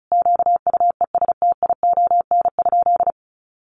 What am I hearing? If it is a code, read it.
Morse code, heard as a keyed tone, about 34 wpm, 700 Hz. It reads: QUESTION?
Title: Sentences from Top 700 Words 35wpm